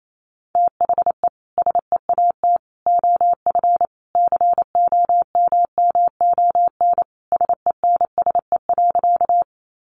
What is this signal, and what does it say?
Morse code, heard as a keyed tone, about 28 wpm, 700 Hz. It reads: THE SEAT OF COMMON SENSE.
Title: Sentences from Top 1500 Words 28wpm